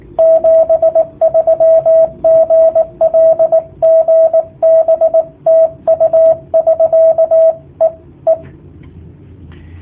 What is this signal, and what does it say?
Morse code, heard as a keyed tone, about 18 wpm, 650 Hz, with radio noise. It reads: 73 GL GB TU <SK> E E
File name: jh8cbh / 2019cw / sp18- w